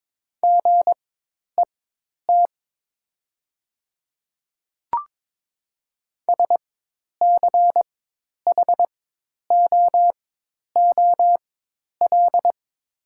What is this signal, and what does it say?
Morse code, heard as a keyed tone, about 22 wpm, spaced as for 10 wpm, 700 Hz. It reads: GET SCHOOL
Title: 4x Character Spacing - Top 300 Words - 22wpm